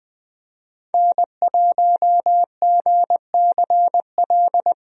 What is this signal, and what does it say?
Morse code, heard as a keyed tone, about 20 wpm, 700 Hz. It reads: N1GCL
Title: USA Call Signs - 5 Characters - 20wpm